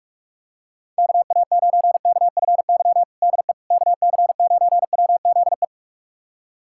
Morse code, heard as a keyed tone, about 46 wpm, 690 Hz. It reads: KA9KFY DE KC9WZE
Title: USA Callsign DE Callsign - 6 Characters - 45wpm